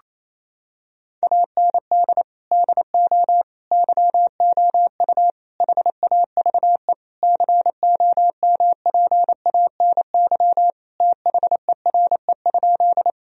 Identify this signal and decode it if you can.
Morse code, heard as a keyed tone, about 28 wpm, 700 Hz. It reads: AND DO YOU HAVE COMPANY THERE?
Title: Sentences from Top 1000 Words - Review Part 3 - 28wpm